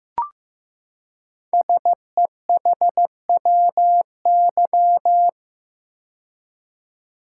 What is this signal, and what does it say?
Morse code, heard as a keyed tone, about 15 wpm, 700 Hz. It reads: SEHWY
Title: Code Groups - 5 Letters - 15wpm